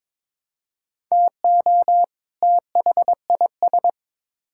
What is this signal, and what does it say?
Morse code, heard as a keyed tone, about 22 wpm, 700 Hz. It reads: TO THIS
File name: Build A Phrase - 2-Word Phrases - 22wpm